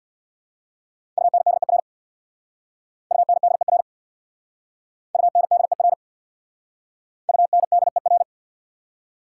Morse code, heard as a keyed tone, about 67 wpm, 690 Hz. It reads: UNDER UNDER UNDER UNDER